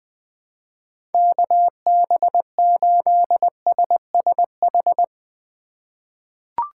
Morse code, heard as a keyed tone, about 20 wpm, 700 Hz. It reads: KB8SSH
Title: 2023 POTA Hunter Call Signs - 20wpm